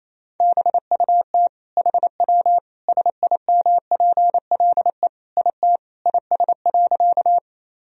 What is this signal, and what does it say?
Morse code, heard as a keyed tone, about 28 wpm, 700 Hz. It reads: BUT HW SIMPLE IT IS.
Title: Sentences from Top 400 Words and Abbreviations 28wpm